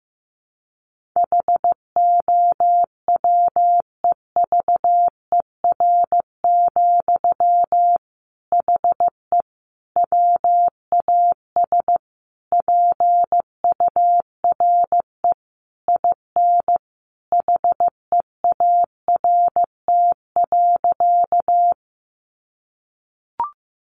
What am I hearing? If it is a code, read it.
Morse code, heard as a keyed tone, about 15 wpm, 700 Hz. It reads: HOWEVER, HE WAS PURE IN HEART.
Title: Sentences from Top 1500 Words 15wpm